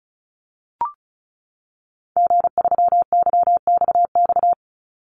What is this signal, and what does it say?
Morse code, heard as a keyed tone, about 34 wpm, 700 Hz. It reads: G3YXX
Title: Call Signs - 35wpm